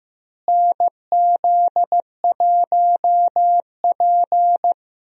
Morse code, heard as a keyed tone, about 15 wpm, 700 Hz. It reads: NZ1P
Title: USA Call Signs - 4 Characters - Repeated 3x - 15wpm